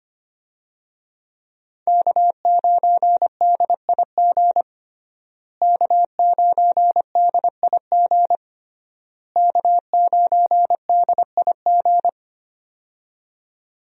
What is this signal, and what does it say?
Morse code, heard as a keyed tone, about 25 wpm, 700 Hz. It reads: K9DIG K9DIG K9DIG
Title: USA Call Signs - 5 Characters - Repeated 3x - 25wpm